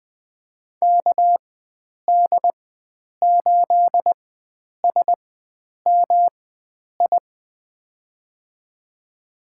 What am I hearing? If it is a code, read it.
Morse code, heard as a keyed tone, about 20 wpm, spaced as for 9 wpm, 700 Hz. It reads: KD8SMI
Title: USA Call Signs - 6 Characters - 4x Character Spacing - 20wpm